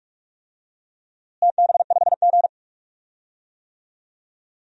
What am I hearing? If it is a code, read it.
Morse code, heard as a keyed tone, about 41 wpm, 700 Hz. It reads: TB5G
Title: DXCC-100 Call Signs - 4 Characters - Repeated 2x - 45wpm